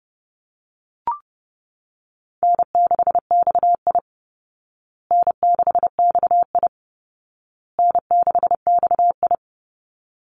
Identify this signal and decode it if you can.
Morse code, heard as a keyed tone, about 30 wpm, 700 Hz. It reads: N6XI N6XI N6XI
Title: Easy Call Signs Repeated 3x - 30wpm